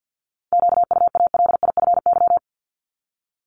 Morse code, heard as a keyed tone, about 50 wpm, 700 Hz. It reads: QUALIFY
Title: Top 1751-2000 Words 50wpm